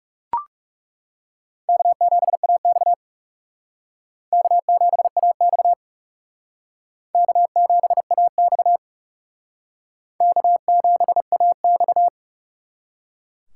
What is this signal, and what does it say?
Morse code, heard as a keyed tone, about 37 wpm, 700 Hz. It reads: K7AX K7AX K7AX K7AX
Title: Speed Racing - USA 1x2 Call Signs 30wpm